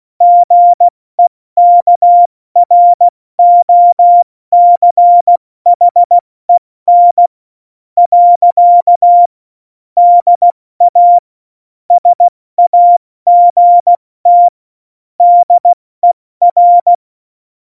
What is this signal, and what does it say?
Morse code, heard as a keyed tone, about 16 wpm, 700 Hz. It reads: GEKROCHEN . DA SAGT DER
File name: Morsekurs / Witze / Witz 13 March 2018, 70 F80